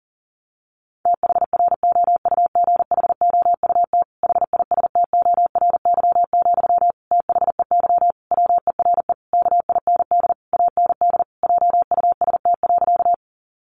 Morse code, heard as a keyed tone, about 40 wpm, 700 Hz. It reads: THROUGHOUT HISTORY, THEY WERE KIND AND JUST.
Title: Sentences from Top 800 Words 40wpm